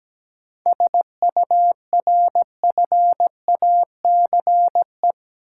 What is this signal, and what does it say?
Morse code, heard as a keyed tone, about 17 wpm, 700 Hz. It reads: SURFACE
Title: Sentences from Top 600 Words - New Word Drill - 17wpm